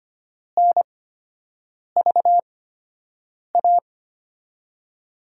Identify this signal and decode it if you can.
Morse code, heard as a keyed tone, about 25 wpm, spaced as for 7 wpm, 700 Hz. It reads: NVA
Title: USA Call Signs - 8x Character Spacing - 25wpm